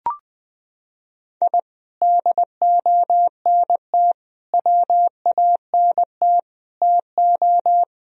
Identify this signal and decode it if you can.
Morse code, heard as a keyed tone, about 20 wpm, 700 Hz. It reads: I DONT WANT TO
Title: Sentences from Top 1000 Words - Review Part 2 - 20wpm